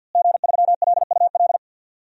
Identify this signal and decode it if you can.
Morse code, heard as a keyed tone, about 47 wpm, 700 Hz. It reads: G3LUL
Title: DXCC-100 Call Signs - 5 Characters - Repeated 3x - 50wpm